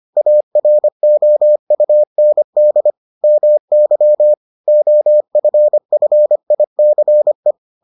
Morse code, heard as a keyed tone, about 25 wpm, 600 Hz. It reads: AROUND MY OFFICE